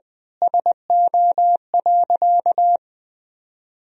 Morse code, heard as a keyed tone, about 20 wpm, 700 Hz. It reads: SO.
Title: Sentences from Top 1000 Words - Review Part 4 - 20wpm